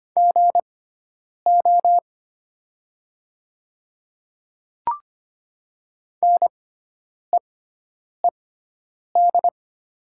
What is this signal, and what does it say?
Morse code, heard as a keyed tone, about 25 wpm, spaced as for 9 wpm, 700 Hz. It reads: GO NEED
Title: 6x Character Spacing - Top 300 Words - 25wpm